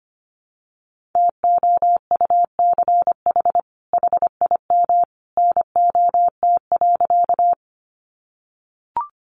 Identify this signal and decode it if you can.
Morse code, heard as a keyed tone, about 25 wpm, 700 Hz. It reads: TOUCH HIM NOT.
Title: Sentences from Top 1200 Words 25wpm